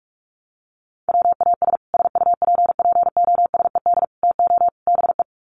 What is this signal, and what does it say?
Morse code, heard as a keyed tone, about 44 wpm, 700 Hz. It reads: WAS SUPPOSED TO BE
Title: Sets of 4 Words 45wpm